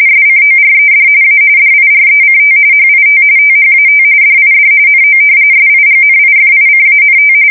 RTTY, about 45.45 baud. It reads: SCORED A DIRECT HIT THAT DID NO SIGNIFICANT
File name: fsk170 / battleshipnj